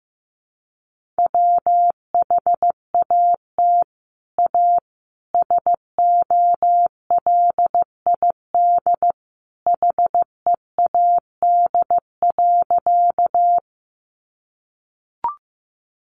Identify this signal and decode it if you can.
Morse code, heard as a keyed tone, about 14 wpm, 700 Hz. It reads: WHAT A SOLID HEAD.
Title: Sentences from Top 1500 Words 15wpm